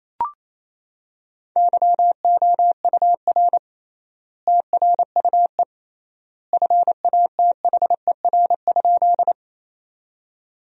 Morse code, heard as a keyed tone, about 28 wpm, 700 Hz. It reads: YOUR TRUE FATHER?
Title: 3x Word-Spacing - Sentences from Top 400 Words - 28wpm